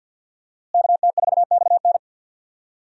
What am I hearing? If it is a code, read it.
Morse code, heard as a keyed tone, about 52 wpm, 690 Hz. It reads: KT4XN
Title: USA Call Signs - 5 Characters - Repeated 3x - 50wpm